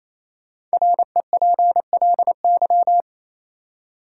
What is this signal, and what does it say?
Morse code, heard as a keyed tone, about 28 wpm, 700 Hz. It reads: REPLY
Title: Five Letter Words - 28wpm